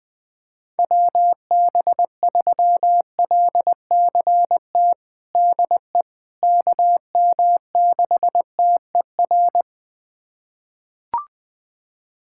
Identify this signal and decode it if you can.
Morse code, heard as a keyed tone, about 20 wpm, 700 Hz. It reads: WB3LCT DE KM6TER
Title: USA Callsign DE Callsign 20wpm